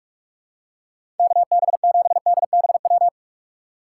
Morse code, heard as a keyed tone, about 46 wpm, 700 Hz. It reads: KB7DBW
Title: USA Call Signs - 6 Characters - 4x Character Spacing - 45wpm